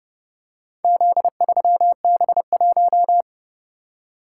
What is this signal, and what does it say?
Morse code, heard as a keyed tone, about 31 wpm, 700 Hz. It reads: Z3B1